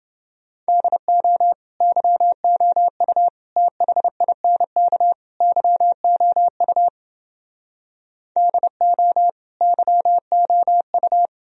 Morse code, heard as a keyed tone, about 30 wpm, 700 Hz. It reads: DO YOU THINK YOU DO YOU